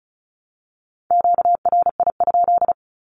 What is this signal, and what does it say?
Morse code, heard as a keyed tone, about 34 wpm, 700 Hz. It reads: QRI?
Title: Q-Codes Repeated 3x 35wpm